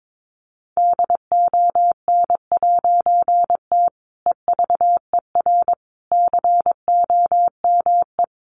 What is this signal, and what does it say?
Morse code, heard as a keyed tone, about 22 wpm, 700 Hz. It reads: DON'T EVER COME
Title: A Sound Of Thunder 22wpm